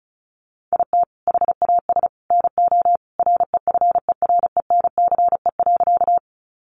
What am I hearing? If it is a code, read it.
Morse code, heard as a keyed tone, about 34 wpm, 700 Hz. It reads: IT HAS NO REFERENCE.
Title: Sentences from Top 1200 Words 35wpm